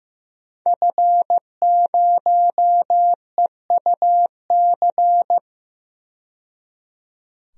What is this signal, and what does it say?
Morse code, heard as a keyed tone, about 15 wpm, 700 Hz. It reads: F0EUC
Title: DXCC-100 Call Signs - 5 Characters - 15wpm